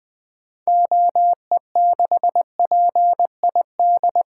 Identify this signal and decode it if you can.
Morse code, heard as a keyed tone, about 20 wpm, 700 Hz. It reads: OE6PID
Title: 2023 SOTA Activator Call Signs - 20wpm